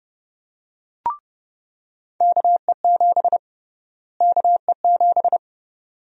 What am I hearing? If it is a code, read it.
Morse code, heard as a keyed tone, about 29 wpm, 700 Hz. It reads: KE7 KE7
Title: USA Call Sign Prefixes - Repeated 2x - 30wpm